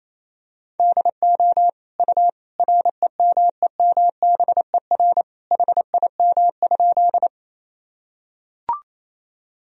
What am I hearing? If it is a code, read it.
Morse code, heard as a keyed tone, about 28 wpm, 700 Hz. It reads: DO U REMEMBER HIM?
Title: Sentences from Top 400 Words and Abbreviations 28wpm